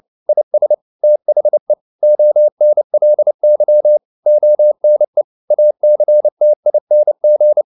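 Morse code, heard as a keyed tone, about 29 wpm, 600 Hz. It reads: IS THE ONLY ONE ACTING
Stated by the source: Morse code at 29 wpm, sending Ken Follett's novel Never in English, from CW med Gnister.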